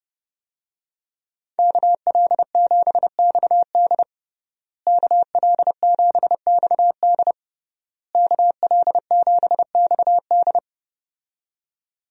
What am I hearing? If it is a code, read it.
Morse code, heard as a keyed tone, about 32 wpm, 700 Hz. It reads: KL7XD KL7XD KL7XD
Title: DXCC-100 Call Signs - 5 Characters - Repeated 3x - 30wpm